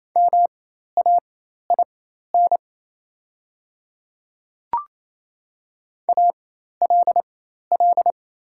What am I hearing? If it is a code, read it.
Morse code, heard as a keyed tone, about 28 wpm, spaced as for 13 wpm, 700 Hz. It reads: MAIN ALL
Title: 4x Character Spacing - Top 500 Words - 28wpm